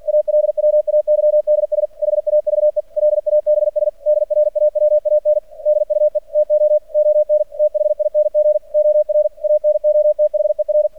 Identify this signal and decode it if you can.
Morse code, heard as a keyed tone, about 48 wpm, 600 Hz. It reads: ALWAYDS HAVE LABS DRAWAN DUE TO ON THING OR ANOTHER
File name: archived / CRSnet